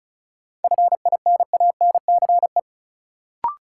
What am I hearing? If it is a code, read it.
Morse code, heard as a keyed tone, about 36 wpm, 700 Hz. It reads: FINANCE
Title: Sentences from Top 1500 Words - New Word Drill Alt - 35wpm